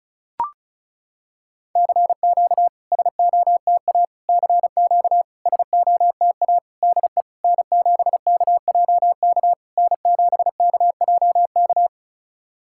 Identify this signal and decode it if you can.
Morse code, heard as a keyed tone, about 36 wpm, 690 Hz. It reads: CQ SOTA CQ SOTA DE N7KJK N7KJK
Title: SOTA QSO Practice 35wpm